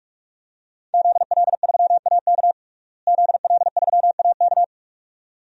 Morse code, heard as a keyed tone, about 46 wpm, 700 Hz. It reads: ZL3AK ZL3AK
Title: DXCC-100 Call Signs - 5 Characters - Repeated 2x - 45wpm